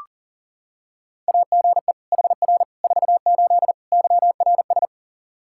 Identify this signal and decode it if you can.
Morse code, heard as a keyed tone, about 40 wpm, 700 Hz. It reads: AGE HR 48 YRS
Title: Practice QSO - Exchange 3 - More Info - 40wpm